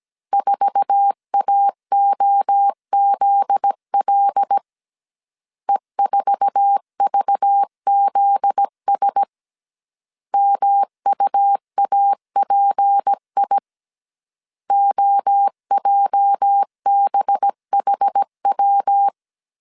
Morse code, heard as a keyed tone, about 17 wpm, 800 Hz. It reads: 4AOZL E4VZS MUAPI OJBHW
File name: CW / DET / 1 9 1500C S C 16wpm 2909